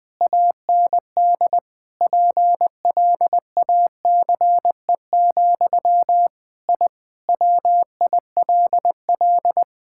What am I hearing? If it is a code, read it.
Morse code, heard as a keyed tone, about 21 wpm, 700 Hz. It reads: AND PLACE, I WILL
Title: Sentences from Top 1000 Words - Review Part 5 - 20wpm